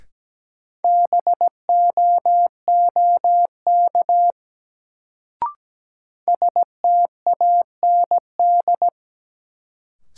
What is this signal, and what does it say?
Morse code, heard as a keyed tone, about 17 wpm, 700 Hz. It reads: BOOK STAND
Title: Sentences from Top 300 Words - New Word Drill - 17wpm